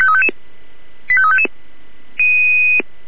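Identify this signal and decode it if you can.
Five-tone selective call (ZVEI), heard as quick runs of five tones, tones of about 70 ms. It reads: 85355
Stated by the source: ZVEI five-tone call 85360 received by SDR.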